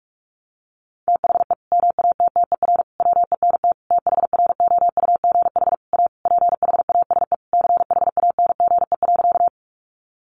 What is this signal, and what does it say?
Morse code, heard as a keyed tone, about 44 wpm, 700 Hz. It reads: THE MATTER WENT THROUGH A PHASE CHANGE.
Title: Sentences from Top 1200 Words 45wpm